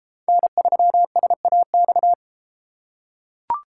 Morse code, heard as a keyed tone, about 33 wpm, 700 Hz. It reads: N3SAX